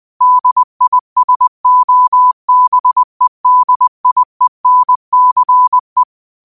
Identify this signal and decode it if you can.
Morse code, heard as a keyed tone, about 20 wpm, 1000 Hz. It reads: DISOBEDIENCE
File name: qotdfiles / 20wpm